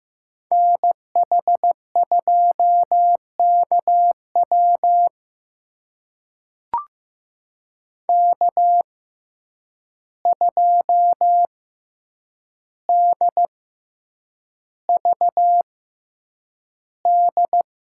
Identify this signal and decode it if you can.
Morse code, heard as a keyed tone, about 15 wpm, spaced as for 8 wpm, 700 Hz. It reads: NH2KW K2DVD
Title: USA Call Signs - 5 Characters - 6x Character Spacing - 15wpm